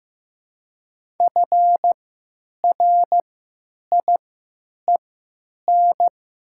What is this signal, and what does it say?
Morse code, heard as a keyed tone, about 15 wpm, spaced as for 9 wpm, 700 Hz. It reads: FRIEN
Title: 3x Character Spacing - Top 300 Words - 15wpm